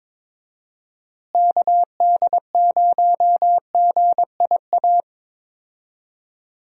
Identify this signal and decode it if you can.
Morse code, heard as a keyed tone, about 22 wpm, 700 Hz. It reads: KD0GIA
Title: USA Call Signs - 6 Characters - Repeated 3x - 22wpm